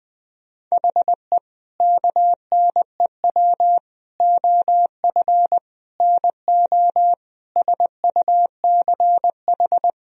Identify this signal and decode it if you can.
Morse code, heard as a keyed tone, about 20 wpm, 700 Hz. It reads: HE KNEW OF NO SUCH
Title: Sentences from Top 1000 Words - Review Part 2 - 20wpm